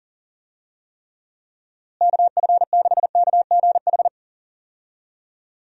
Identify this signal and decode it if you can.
Morse code, heard as a keyed tone, about 40 wpm, 700 Hz. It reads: KF6KGH
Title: USA Call Signs - 6 Characters - Repeated 3x - 40wpm